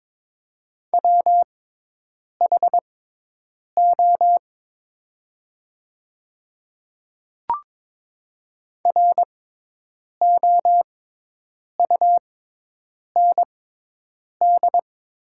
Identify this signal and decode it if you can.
Morse code, heard as a keyed tone, about 22 wpm, spaced as for 8 wpm, 700 Hz. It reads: WHO ROUND
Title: 6x Character Spacing - Top 200 Words - 22wpm